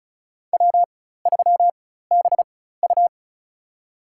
Morse code, heard as a keyed tone, about 34 wpm, spaced as for 16 wpm, 700 Hz. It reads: W3BU
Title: USA Call Signs - 4 Characters - 4x Character Spacing - 35wpm